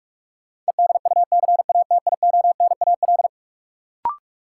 Morse code, heard as a keyed tone, about 41 wpm, 700 Hz. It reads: EDUCATIONAL
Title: Sentences from Top 1200 Words - New Word Drill - 45wpm